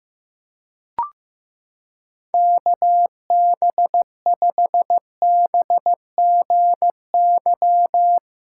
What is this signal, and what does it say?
Morse code, heard as a keyed tone, about 15 wpm, 700 Hz. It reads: KB5BGY
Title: USA Callsign DE Callsign 15wpm